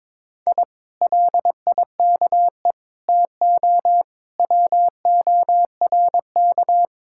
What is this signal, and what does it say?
Morse code, heard as a keyed tone, about 23 wpm, 700 Hz. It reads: I LIKE TO WORK
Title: Speed-Racing Common QSO Phrases - 22wpm